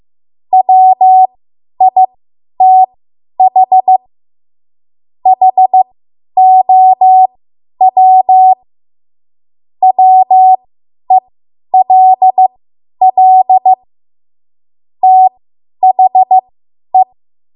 Morse code, heard as a keyed tone, about 15 wpm, spaced as for 10 wpm, 750 Hz. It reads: WITH HOW WELL THE
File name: Morse / 10WPM